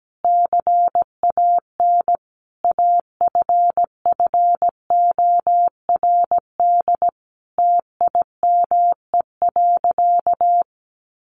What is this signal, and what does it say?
Morse code, heard as a keyed tone, about 17 wpm, 700 Hz. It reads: CAN AFFORD TIME.
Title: Sentences from Top 1100 Words 17wpm